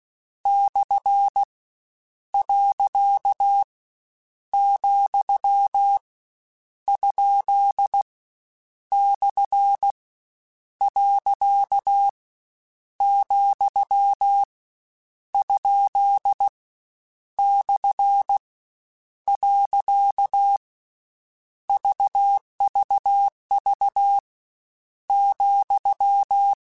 Morse code, heard as a keyed tone, about 16 wpm, 800 Hz. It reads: / . , ? / . , ? / . VVV ,
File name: Misc / Misc-mp3s / Punct @15